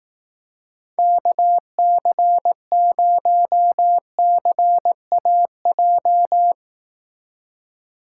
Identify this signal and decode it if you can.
Morse code, heard as a keyed tone, about 18 wpm, 700 Hz. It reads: KC0CAJ
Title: Speed Racing - USA 2x3 Call Signs 15wpm